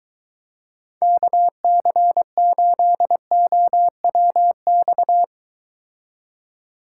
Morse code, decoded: KC8OWX

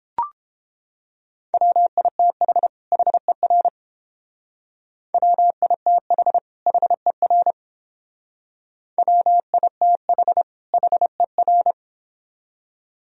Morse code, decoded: WITH HER WITH HER WITH HER

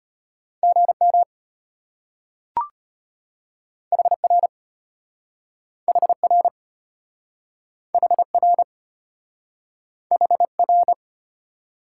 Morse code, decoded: GM HR HR HR HR

